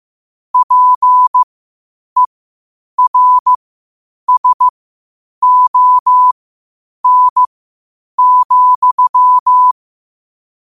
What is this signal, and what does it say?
Morse code, heard as a keyed tone, about 15 wpm, spaced as for 8 wpm, 1000 Hz. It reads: PERSON,
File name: qotdfiles / 8wpm